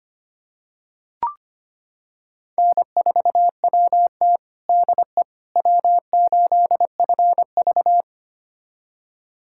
Morse code, heard as a keyed tone, about 25 wpm, 700 Hz. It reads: N4WT DE W8FV
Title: USA Callsign DE Callsign - 4 Characters - 25wpm